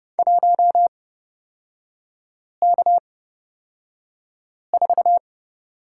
Morse code, heard as a keyed tone, about 30 wpm, spaced as for 5 wpm, 700 Hz. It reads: 1K4
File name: Lesson 36e - BT - Three Characters 30wpm